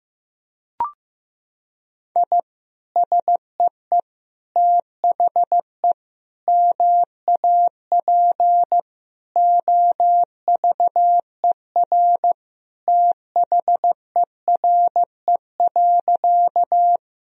Morse code, heard as a keyed tone, about 15 wpm, 700 Hz. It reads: I SEE THE MAP OVER THERE.
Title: Sentences from Top 1000 Words - Review Part 2 - 15wpm